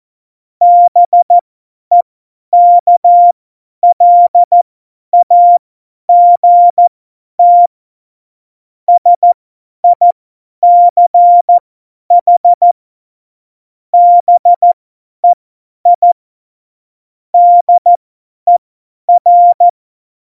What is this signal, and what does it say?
Morse code, heard as a keyed tone, about 14 wpm, spaced as for 10 wpm, 700 Hz. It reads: BEKLAGT SICH BEI DER
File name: Morsekurs / Witze / Witz 15. September 2020, 50 F70